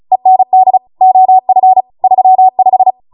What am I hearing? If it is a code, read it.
Morse code, heard as a keyed tone, about 36 wpm, 750 Hz. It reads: END OF 35